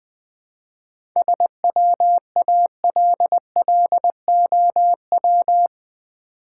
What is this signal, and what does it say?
Morse code, heard as a keyed tone, about 20 wpm, 690 Hz. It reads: SWALLOW
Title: 7 Letter Words - First 3 Letters Given - 20wpm